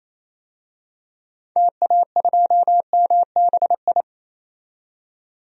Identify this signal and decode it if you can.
Morse code, heard as a keyed tone, about 28 wpm, 700 Hz. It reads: TA2MBI
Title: DXCC-100 Call Signs - 6 Characters - 28wpm